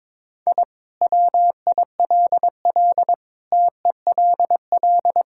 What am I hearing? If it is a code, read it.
Morse code, heard as a keyed tone, about 22 wpm, 700 Hz. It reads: I WILL TELL
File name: practice / 22wpm / Sentences with 4-Word Phrases - 22wpm